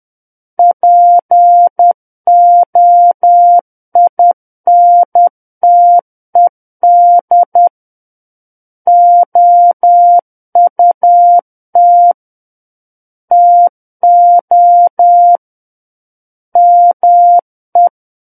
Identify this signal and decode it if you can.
Morse code, heard as a keyed tone, about 10 wpm, 700 Hz. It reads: POINTED OUT TO ME
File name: Book One / 01-Chapter 1 - 10 WPM